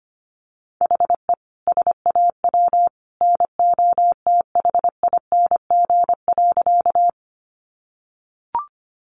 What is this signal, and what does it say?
Morse code, heard as a keyed tone, about 25 wpm, 700 Hz. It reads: HE SAW NOTHING.
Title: Sentences from Top 500 Words 25wpm